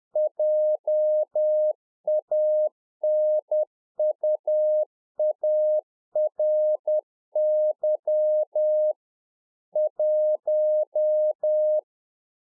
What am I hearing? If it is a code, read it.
Morse code, heard as a keyed tone, about 10 wpm, 610 Hz. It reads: JANUARY 1
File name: 2011-04-mp3 / A SMALL TRANCEIVER 1 10WPM